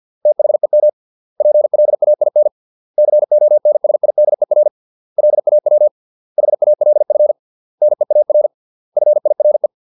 Morse code, heard as a keyed tone, about 50 wpm, 600 Hz. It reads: THEM PLAIN CONSIDER LAW HALF NEAR FIRE